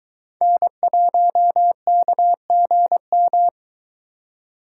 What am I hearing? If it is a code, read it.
Morse code, heard as a keyed tone, about 22 wpm, 700 Hz. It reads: N1KGM